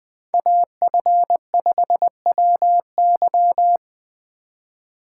Morse code, heard as a keyed tone, about 20 wpm, 700 Hz. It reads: AF5WY
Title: Speed Racing - USA 2x2 Call Signs 17wpm